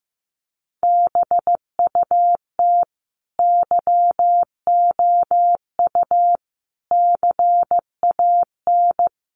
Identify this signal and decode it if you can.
Morse code, heard as a keyed tone, about 15 wpm, 700 Hz. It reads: BUT YOU CAN
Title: Sentences from Top 1500 Words 15wpm